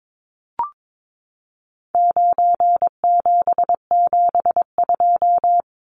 Morse code, heard as a keyed tone, about 22 wpm, 700 Hz. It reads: 9772
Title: Numbers - 4-Digit 22wpm